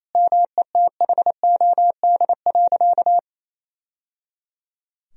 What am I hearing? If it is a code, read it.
Morse code, heard as a keyed tone, about 28 wpm, 700 Hz. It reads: METHOD.
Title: Sentences from Top 1000 Words - Review Part 3 - 28wpm